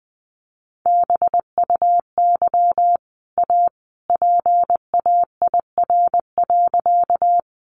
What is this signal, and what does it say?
Morse code, heard as a keyed tone, about 20 wpm, 700 Hz. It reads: BUY A PAIR.
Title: Sentences from Top 1200 Words 20wpm